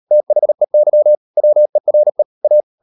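Morse code, heard as a keyed tone, about 38 wpm, 600 Hz. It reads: THEY WERE A
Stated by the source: Morse code practice at 38 WPM, sending Ken Follett's novel Never in English, from CW med Gnister.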